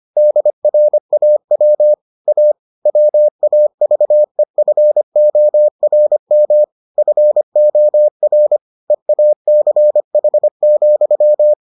Morse code, heard as a keyed tone, about 25 wpm, 600 Hz. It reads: DRAW A WAVEFORM FOR EACH,